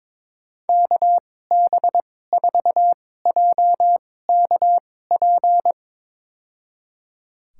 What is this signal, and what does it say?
Morse code, heard as a keyed tone, about 22 wpm, spaced as for 16 wpm, 700 Hz. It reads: KB4JKP
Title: USA Call Signs - 6 Characters - 2x Character Spacing - 22wpm